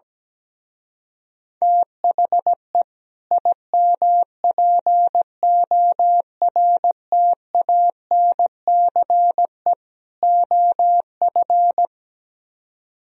Morse code, heard as a keyed tone, about 17 wpm, 700 Hz. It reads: THE IMPORTANCE OF
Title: Speed-Racing - Sets of 3 Words 17wpm